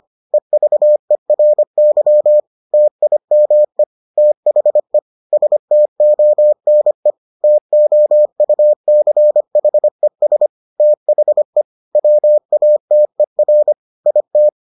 Morse code, heard as a keyed tone, about 25 wpm, 600 Hz. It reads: EVERY TIME THE STONE TOUCHES THE WATER IT